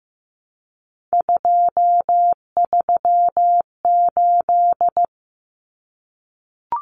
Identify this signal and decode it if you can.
Morse code, heard as a keyed tone, about 15 wpm, 700 Hz. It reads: 238
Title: Numbers - 3-Digit 15wpm